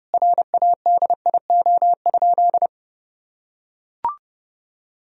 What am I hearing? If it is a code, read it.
Morse code, heard as a keyed tone, about 32 wpm, 700 Hz. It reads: RADIO?
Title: Sentences from Top 1000 Words - Review Part 3 - 30wpm